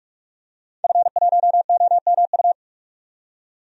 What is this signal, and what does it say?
Morse code, heard as a keyed tone, about 46 wpm, 700 Hz. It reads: F1OMU
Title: DXCC-100 Call Signs - 5 Characters - 45wpm